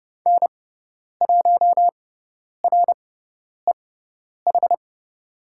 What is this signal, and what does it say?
Morse code, heard as a keyed tone, about 29 wpm, spaced as for 10 wpm, 690 Hz. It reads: N1REH
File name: Lesson 18g - Letter m - Call Signs 30wpm